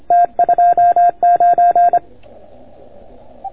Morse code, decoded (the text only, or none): T29